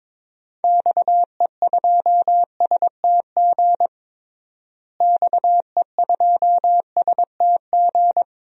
Morse code, heard as a keyed tone, about 22 wpm, 700 Hz. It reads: XE2STG XE2STG
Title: DXCC-100 Call Signs - 6 Characters - Repeated 2x - 22wpm